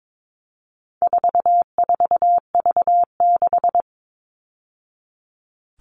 Morse code, heard as a keyed tone, about 22 wpm, 700 Hz. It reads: 44V6
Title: All Cut Numbers - 4-Digit 22wpm